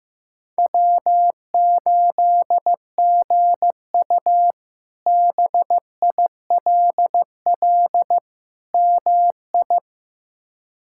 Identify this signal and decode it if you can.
Morse code, heard as a keyed tone, about 16 wpm, 700 Hz. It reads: W8GU BILL MI
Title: K1USN SST QSO Practice - 15wpm